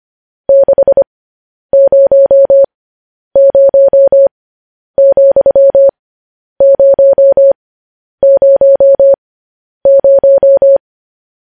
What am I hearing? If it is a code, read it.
Morse code, heard as a keyed tone, about 25 wpm, spaced as for 10 wpm, 550 Hz. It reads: 600,000
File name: podcast / News headlines-1x-25x10